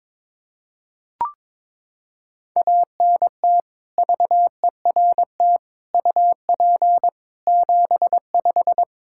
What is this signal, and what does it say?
Morse code, heard as a keyed tone, about 22 wpm, 700 Hz. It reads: ANT VERT UP 75
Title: K1USN SST QSO Practice 22wpm